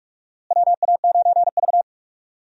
Morse code, heard as a keyed tone, about 44 wpm, 700 Hz. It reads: WA9V
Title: USA Call Signs - 4 Characters - Repeated 2x - 45wpm